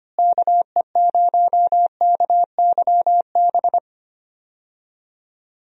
Morse code, heard as a keyed tone, about 25 wpm, 700 Hz. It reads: KE0KYB
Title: USA Call Signs - 6 Characters - Repeated 3x - 25wpm